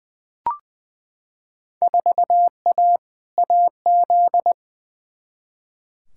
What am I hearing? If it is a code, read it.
Morse code, heard as a keyed tone, about 20 wpm, 700 Hz. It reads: 4A AZ
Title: Field Day Exchanges Encore - 20wpm